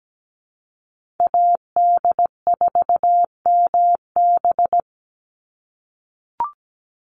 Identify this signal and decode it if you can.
Morse code, heard as a keyed tone, about 17 wpm, 700 Hz. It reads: AD4MB
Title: USA Call Signs 17wpm